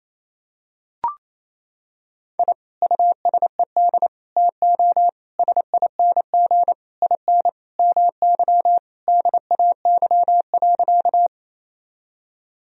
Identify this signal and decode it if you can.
Morse code, heard as a keyed tone, about 28 wpm, 700 Hz. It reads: I USED TO SING IN MY DAY.
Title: Sentences from Top 1000 Words - Review Part 2 - 28wpm